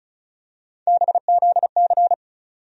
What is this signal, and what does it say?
Morse code, heard as a keyed tone, about 36 wpm, 700 Hz. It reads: BZC
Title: USA Call Sign Postfixes - 3 Letters 35wpm